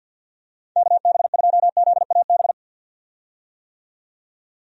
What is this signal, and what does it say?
Morse code, heard as a keyed tone, about 49 wpm, 700 Hz. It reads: KB2ZAB
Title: USA Call Signs - 6 Characters - 50wpm